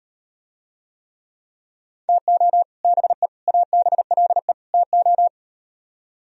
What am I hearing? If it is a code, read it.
Morse code, heard as a keyed tone, about 38 wpm, 700 Hz. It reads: TO BE ABLE TO